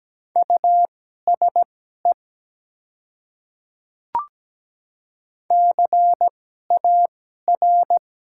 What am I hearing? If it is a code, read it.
Morse code, heard as a keyed tone, about 17 wpm, spaced as for 12 wpm, 700 Hz. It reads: USE CAR
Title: 2x Character Spacing - Top 300 Words - 17wpm